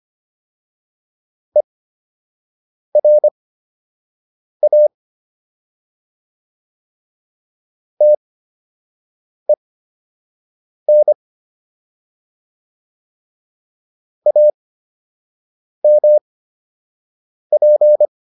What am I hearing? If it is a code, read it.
Morse code, heard as a keyed tone, about 25 wpm, spaced as for 6 wpm, 600 Hz. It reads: ERA TEN AMP